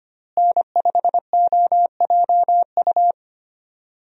Morse code, heard as a keyed tone, about 25 wpm, 700 Hz. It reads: N5OJU
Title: USA Call Signs - 5 Characters - Repeated 2x - 25wpm